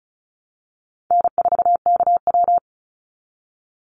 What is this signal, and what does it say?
Morse code, heard as a keyed tone, about 36 wpm, 700 Hz. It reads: N4KW
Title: Call Signs - 35wpm